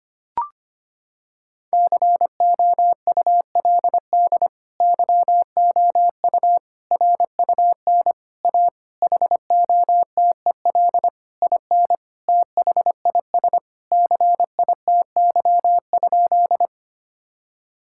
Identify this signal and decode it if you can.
Morse code, heard as a keyed tone, about 25 wpm, 700 Hz. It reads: COULD YOU RUN A HOTEL IN THIS CITY?
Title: Sentences from Top 1600 Words 25wpm